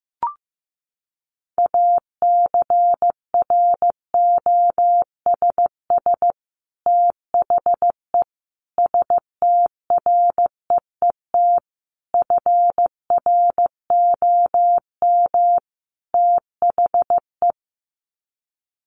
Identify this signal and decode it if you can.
Morse code, heard as a keyed tone, about 15 wpm, 700 Hz. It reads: ACROSS THE STREET FROM THE
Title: Sets of 5 Words Encore 15wpm